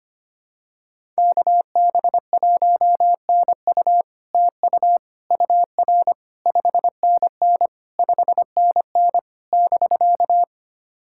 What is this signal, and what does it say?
Morse code, heard as a keyed tone, about 25 wpm, 700 Hz. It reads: KB1NU TU UR 5NN 5NN <BK>
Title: SOTA QSO Practice 25wpm